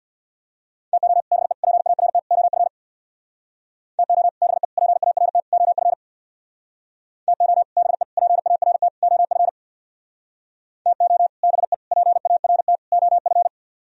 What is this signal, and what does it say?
Morse code, heard as a keyed tone, about 61 wpm, 700 Hz. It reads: TO BE PART OF TO BE PART OF TO BE PART OF TO BE PART OF
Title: Speed-Racing - Sets of 4 Words 50wpm